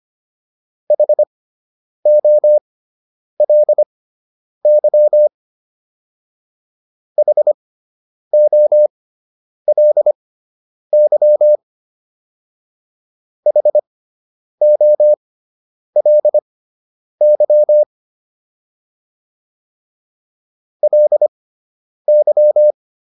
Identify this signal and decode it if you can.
Morse code, heard as a keyed tone, about 25 wpm, spaced as for 9 wpm, 600 Hz. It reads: HOLY HOLY HOLY LY